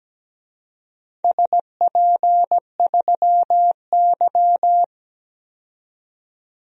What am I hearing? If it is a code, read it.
Morse code, heard as a keyed tone, about 17 wpm, 700 Hz. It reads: SP3Y
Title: DXCC-100 Call Signs - 4 Characters - 17wpm